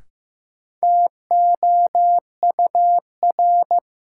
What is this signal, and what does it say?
Morse code, heard as a keyed tone, about 16 wpm, 700 Hz. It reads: TOUR
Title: Sentences from Top 1400 Words - New Word Drill Alt - 15wpm